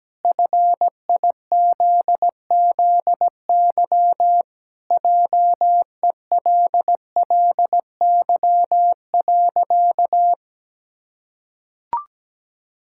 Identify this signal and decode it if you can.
Morse code, heard as a keyed tone, about 17 wpm, 700 Hz. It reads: FIZZY JELLY.
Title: Pangram Challenge v2 - 17wpm